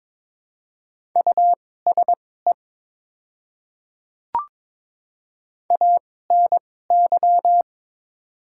Morse code, heard as a keyed tone, about 21 wpm, spaced as for 16 wpm, 700 Hz. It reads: USE ANY
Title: 2x Character Spacing - Top 200 Words - 22wpm